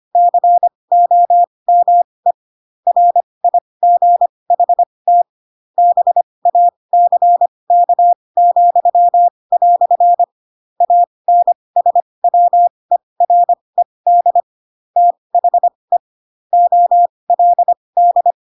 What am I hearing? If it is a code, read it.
Morse code, heard as a keyed tone, about 25 wpm, spaced as for 20 wpm, 700 Hz. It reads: COME RIGHT BACK," ANSWERED THE OLD
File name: Pinocchio / Pinocchio ch6 15 25 ik1hge